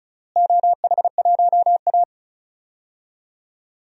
Morse code, heard as a keyed tone, about 36 wpm, 700 Hz. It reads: OH1A
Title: DXCC-100 Call Signs - 4 Characters - Repeated 3x - 35wpm